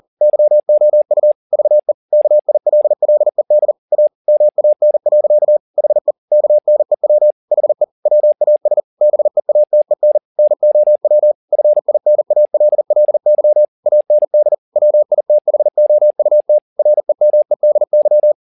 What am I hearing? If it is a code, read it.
Morse code, heard as a keyed tone, about 40 wpm, 600 Hz. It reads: YOU VE KILLED A MAN. HE KNEW HE WAS BEATEN NOW FINALLY AND WITHOUT REMEDY
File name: the old man and the sea 2 40wpm